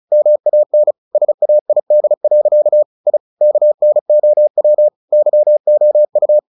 Morse code, heard as a keyed tone, about 34 wpm, 600 Hz. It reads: MAN SAID. I KNOW YOU
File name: the old man and the sea / the old man and the sea 1 35wpm